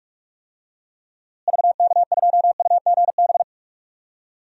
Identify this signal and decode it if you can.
Morse code, heard as a keyed tone, about 43 wpm, 700 Hz. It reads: VK2UGB
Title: DXCC-100 Call Signs - 6 Characters - Repeated 3x - 45wpm